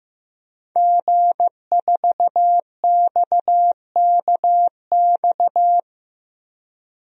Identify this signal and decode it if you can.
Morse code, heard as a keyed tone, about 15 wpm, 700 Hz. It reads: G4XKX
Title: DXCC-100 Call Signs - 5 Characters - 15wpm